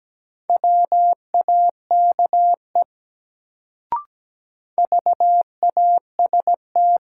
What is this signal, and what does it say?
Morse code, heard as a keyed tone, about 17 wpm, 700 Hz. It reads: WAKE VAST
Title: Sets of 4 Words - New Word Drill - 17wpm